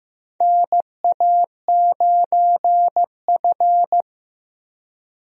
Morse code, heard as a keyed tone, about 15 wpm, 700 Hz. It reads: NA9F
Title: USA Call Signs - 4 Characters - Repeated 2x - 15wpm